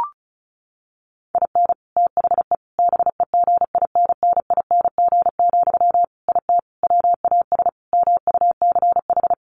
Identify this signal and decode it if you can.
Morse code, heard as a keyed tone, about 34 wpm, 700 Hz. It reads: IN THE BEGINNING, IT WAS MUCH
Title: Sentences from Top 1400 Words 35wpm